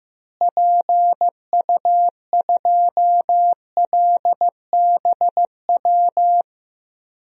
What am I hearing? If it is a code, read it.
Morse code, heard as a keyed tone, about 16 wpm, 700 Hz. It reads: PU2LBW
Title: DXCC-100 Call Signs - 6 Characters - Repeated 3x - 15wpm